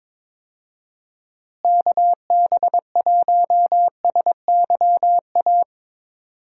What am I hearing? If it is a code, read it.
Morse code, heard as a keyed tone, about 22 wpm, 700 Hz. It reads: KB1SYA
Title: USA Call Signs - 6 Characters - Repeated 2x - 22wpm